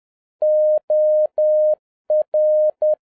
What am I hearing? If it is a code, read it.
Morse code, heard as a keyed tone, about 10 wpm, 610 Hz. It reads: OR
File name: 2011-07-mp3 / n56 BACK FROM A BIG ACCIDENT 10wpm